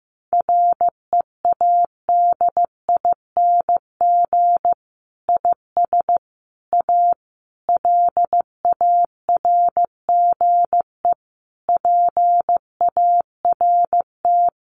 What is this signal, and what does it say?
Morse code, heard as a keyed tone, about 15 wpm, 700 Hz. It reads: READING IS A LARGE PART
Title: Sentences from Top 1400 Words 15wpm